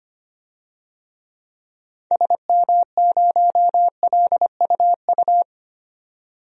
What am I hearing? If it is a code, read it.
Morse code, heard as a keyed tone, about 25 wpm, 700 Hz. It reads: SM0LUU
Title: DXCC-100 Call Signs - 6 Characters - 25wpm